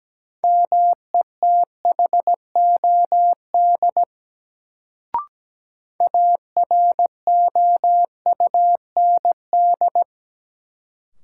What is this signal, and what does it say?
Morse code, heard as a keyed tone, about 17 wpm, 700 Hz. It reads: METHOD AROUND
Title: Sentences from Top 600 Words - New Word Drill - 17wpm